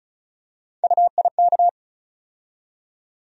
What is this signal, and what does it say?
Morse code, heard as a keyed tone, about 36 wpm, 700 Hz. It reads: UIK
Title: USA Call Sign Postfixes - 3 Letters 35wpm